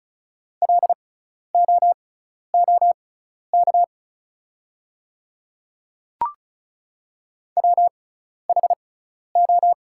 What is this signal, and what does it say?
Morse code, heard as a keyed tone, about 36 wpm, spaced as for 12 wpm, 700 Hz. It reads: LOOK WHO
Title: 6x Character Spacing - Top 100 Words - 35wpm